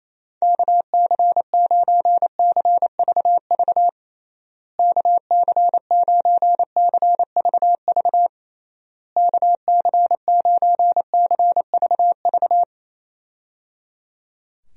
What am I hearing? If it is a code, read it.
Morse code, heard as a keyed tone, about 28 wpm, 700 Hz. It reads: KC9CVV KC9CVV KC9CVV
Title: USA Call Signs - 6 Characters - Repeated 3x - 28wpm